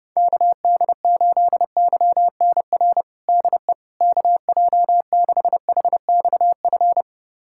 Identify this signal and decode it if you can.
Morse code, heard as a keyed tone, about 30 wpm, 700 Hz. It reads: KD8YNR DE KJ6HXF
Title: USA Callsign DE Callsign 30wpm